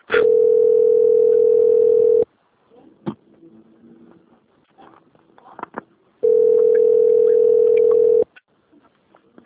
Ringback tone (North America), 440 + 480 Hz.